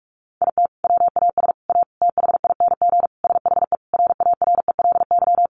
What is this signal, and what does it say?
Morse code, heard as a keyed tone, about 44 wpm, 700 Hz. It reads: IT WAS A THING SHE RARELY
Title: Sentences from Top 1400 Words 45wpm